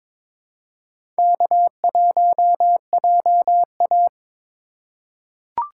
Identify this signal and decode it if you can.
Morse code, heard as a keyed tone, about 22 wpm, 700 Hz. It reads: K1JA
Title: USA Call Signs - 4 Characters - 3x Character Spacing - 22wpm